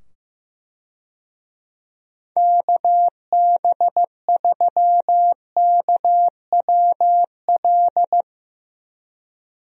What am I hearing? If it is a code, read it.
Morse code, heard as a keyed tone, about 15 wpm, 700 Hz. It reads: KB3KWL
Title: USA Call Signs - 6 Characters - 15wpm